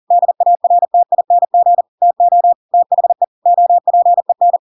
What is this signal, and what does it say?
Morse code, heard as a keyed tone, about 40 wpm, 700 Hz. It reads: DARTING TO THE OPEN
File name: Pinocchio / Pinocchio ch5 40 40 ik1hge